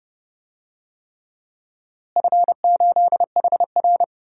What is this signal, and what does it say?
Morse code, heard as a keyed tone, about 30 wpm, 700 Hz. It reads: F8HR
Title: DXCC-100 Call Signs - 4 Characters - Repeated 3x - 30wpm